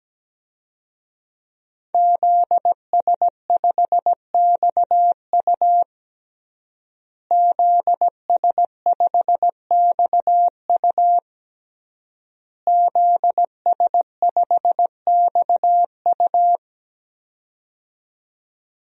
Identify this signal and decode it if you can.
Morse code, heard as a keyed tone, about 17 wpm, 700 Hz. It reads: ZS5XU ZS5XU ZS5XU
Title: DXCC-100 Call Signs - 5 Characters - Repeated 3x - 17wpm